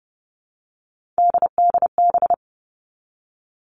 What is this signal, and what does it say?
Morse code, heard as a keyed tone, about 30 wpm, 700 Hz. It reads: DDB
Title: All Cut Numbers - 3-Digit 30wpm